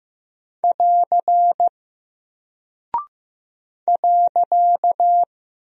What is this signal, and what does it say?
Morse code, heard as a keyed tone, about 15 wpm, 700 Hz. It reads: <AR> .
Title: Single Character (Letter:Number:Punctuation:Prosign) Alt 15wpm